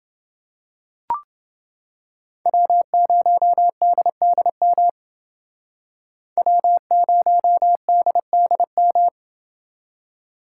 Morse code, heard as a keyed tone, about 28 wpm, 700 Hz. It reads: W0DDM W0DDM